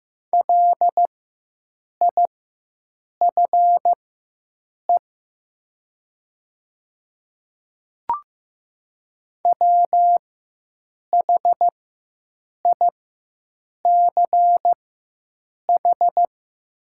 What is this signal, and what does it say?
Morse code, heard as a keyed tone, about 15 wpm, spaced as for 7 wpm, 700 Hz. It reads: LIFE WHICH
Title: 4x Character Spacing - Top 300 Words - 15wpm